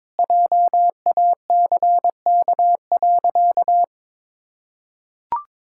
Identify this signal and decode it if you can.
Morse code, heard as a keyed tone, about 22 wpm, 700 Hz. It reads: JACK.
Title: Pangram Challenge v2 - 22wpm